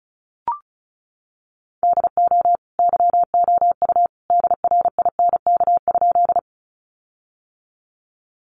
Morse code, heard as a keyed tone, about 34 wpm, 700 Hz. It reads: DO YOU DRINK?
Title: Sentences from Top 1500 Words 35wpm